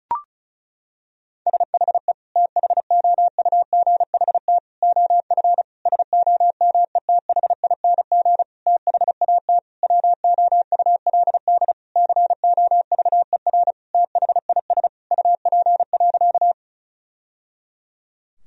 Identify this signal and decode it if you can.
Morse code, heard as a keyed tone, about 34 wpm, 700 Hz. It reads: SHE THOUGHT OF SOMETHING THAT WOULD COVER THIS UP.
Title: Sentences from Top 1000 Words - Review Part 1 - 35wpm